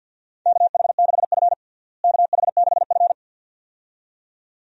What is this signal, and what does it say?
Morse code, heard as a keyed tone, about 52 wpm, 700 Hz. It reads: KH6F KH6F
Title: USA Call Signs - 4 Characters - Repeated 2x - 50wpm